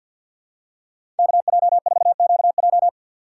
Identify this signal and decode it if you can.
Morse code, heard as a keyed tone, about 52 wpm, 700 Hz. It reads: KJ4QJ
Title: USA Call Signs - 2x Character Spacing - 50wpm